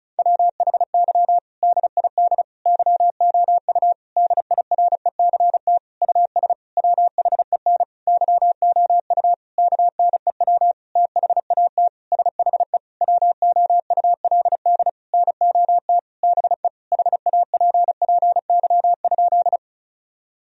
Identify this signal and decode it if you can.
Morse code, heard as a keyed tone, about 34 wpm, 700 Hz. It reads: WHY DID YOU DIRECT US WHEN YOU KNEW THAT SHE WOULD NOT BE HAPPY?
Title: Sentences from Top 1000 Words - Review Part 2 - 35wpm